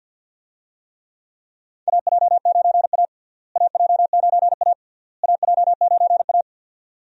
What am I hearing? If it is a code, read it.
Morse code, heard as a keyed tone, about 50 wpm, 700 Hz. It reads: AJ9A AJ9A AJ9A
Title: USA Call Signs - 4 Characters - Repeated 3x - 50wpm